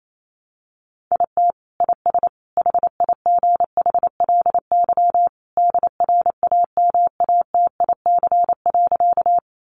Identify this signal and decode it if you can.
Morse code, heard as a keyed tone, about 28 wpm, 700 Hz. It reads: IT IS HIGHLY DRAMATIC.
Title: Sentences from Top 1300 Words 28wpm